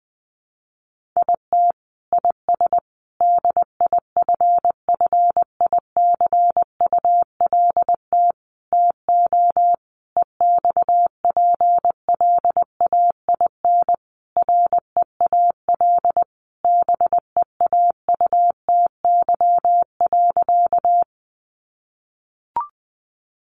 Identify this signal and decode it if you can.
Morse code, heard as a keyed tone, about 20 wpm, 700 Hz. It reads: IT IS DIFFICULT TO EXPLAIN REAL BEAUTY.
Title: Sentences from Top 1100 Words 20wpm